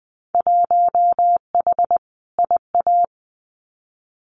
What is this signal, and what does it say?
Morse code, heard as a keyed tone, about 20 wpm, 700 Hz. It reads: 1H IA
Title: Winter Field Day Exchanges 20wpm